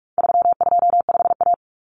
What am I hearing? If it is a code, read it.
Morse code, heard as a keyed tone, about 44 wpm, 700 Hz. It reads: 325A
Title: Common Cut Numbers - 4-Digit 45wpm